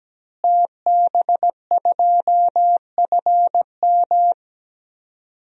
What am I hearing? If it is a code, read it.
Morse code, heard as a keyed tone, about 17 wpm, 700 Hz. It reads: TB2FM
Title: DXCC-100 Call Signs - 5 Characters - Repeated 2x - 17wpm